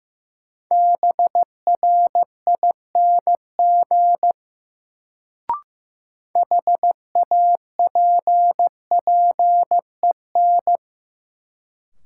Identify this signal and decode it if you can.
Morse code, heard as a keyed tone, about 16 wpm, 700 Hz. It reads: BRING HAPPEN
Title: Sets of 5 Words Encore - New Word Drill - 15wpm